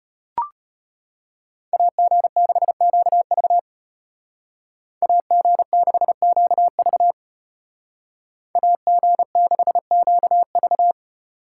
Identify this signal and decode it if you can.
Morse code, heard as a keyed tone, about 33 wpm, 700 Hz. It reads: AG6QV AG6QV AG6QV